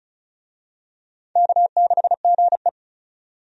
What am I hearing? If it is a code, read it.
Morse code, heard as a keyed tone, about 34 wpm, 700 Hz. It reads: K6GE
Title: USA Call Signs - 4 Characters - 35wpm